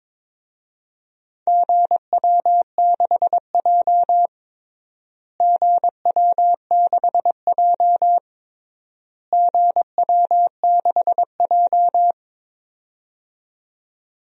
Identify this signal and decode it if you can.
Morse code, heard as a keyed tone, about 22 wpm, 700 Hz. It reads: GW6J GW6J GW6J
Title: DXCC-100 Call Signs - 4 Characters - Repeated 3x - 22wpm